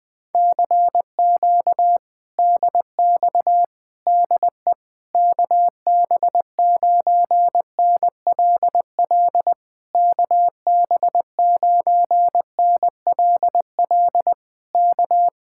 Morse code, heard as a keyed tone, about 19 wpm, 700 Hz. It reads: CQ DX DE KB9NLL KB9NLL K
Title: Speed-Racing Common QSO Phrases - 20wpm